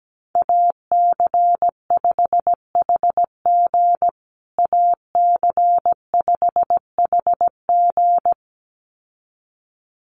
Morse code, heard as a keyed tone, about 17 wpm, 700 Hz. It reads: AC5HG AC5HG
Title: USA Call Signs Repeated 2x 17wpm